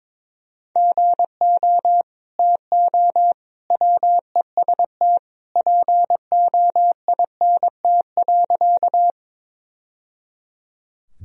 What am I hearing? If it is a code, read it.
Morse code, heard as a keyed tone, about 22 wpm, 700 Hz. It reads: GO TO WEST POINT.
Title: Sentences from Top 1000 Words - Review Part 2 - 22wpm